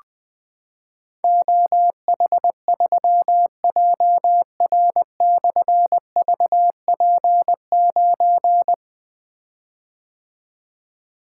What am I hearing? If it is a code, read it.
Morse code, heard as a keyed tone, about 20 wpm, 700 Hz. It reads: OH3JR/VP9
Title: Difficult Call Signs 20wpm